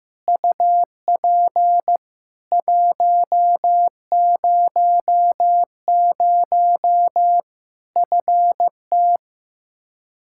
Morse code, heard as a keyed tone, about 15 wpm, 700 Hz. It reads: UP 100 FT
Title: Practice QSO - Exchange 2 - Info - 15wpm